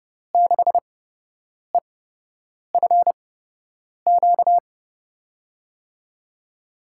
Morse code, heard as a keyed tone, about 32 wpm, spaced as for 8 wpm, 700 Hz. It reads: 6EFQ
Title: USA Call Signs - 6 Characters - 8x Character Spacing - 30wpm